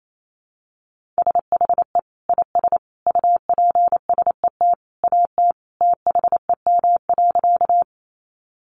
Morse code, heard as a keyed tone, about 28 wpm, 700 Hz. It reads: SHE IS UPSET AT THEM.
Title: Sentences from Top 800 Words 28wpm